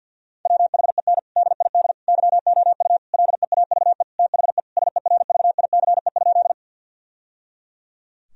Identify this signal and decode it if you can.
Morse code, heard as a keyed tone, about 50 wpm, 700 Hz. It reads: WHEN DID YOU LEAVE THE SERVICE?
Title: Sentences from Top 1000 Words - Review Part 4 - 50wpm